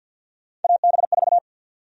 Morse code, heard as a keyed tone, about 50 wpm, 700 Hz. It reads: AB4
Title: USA Call Sign Prefixes - Repeated 2x - 50wpm